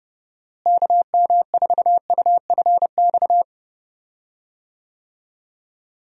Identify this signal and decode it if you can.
Morse code, heard as a keyed tone, about 30 wpm, 700 Hz. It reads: KM4UFX
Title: USA Call Signs - 6 Characters - Repeated 2x - 30wpm